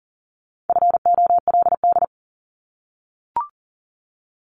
Morse code, heard as a keyed tone, about 40 wpm, 690 Hz. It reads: FOLD
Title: Multi-letter Phonemes Repeated 3x 40wpm